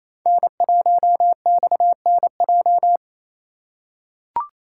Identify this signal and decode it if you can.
Morse code, heard as a keyed tone, about 28 wpm, 700 Hz. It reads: N1XNJ
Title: USA Call Signs - 5 Characters - 4x Character Spacing - 28wpm